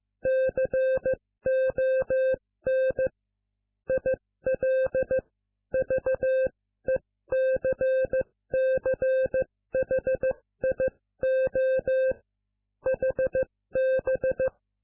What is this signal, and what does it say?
Morse code, heard as a keyed tone, about 15 wpm, spaced as for 13 wpm, 520 Hz. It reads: CON ILVECCHIO HB